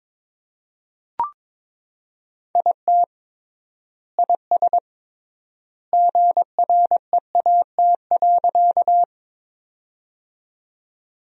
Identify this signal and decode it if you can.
Morse code, heard as a keyed tone, about 22 wpm, 700 Hz. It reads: IT IS GREAT.
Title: 3x Word-Spacing - Sentences from Top 200 Words - 22wpm